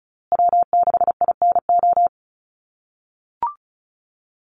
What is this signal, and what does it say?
Morse code, heard as a keyed tone, about 34 wpm, 700 Hz. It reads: W6INO
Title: USA Call Signs Repeated 2x 35wpm